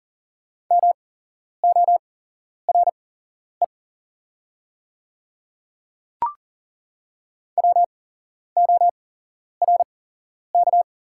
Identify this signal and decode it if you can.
Morse code, heard as a keyed tone, about 40 wpm, spaced as for 11 wpm, 700 Hz. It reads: MORE WORK
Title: 8x Character Spacing - Top 500 Words - 40wpm